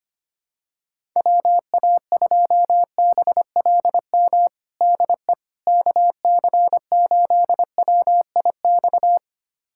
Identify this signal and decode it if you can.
Morse code, heard as a keyed tone, about 25 wpm, 700 Hz. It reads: WA2BLM DE KC8WIX
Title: USA Callsign DE Callsign 25wpm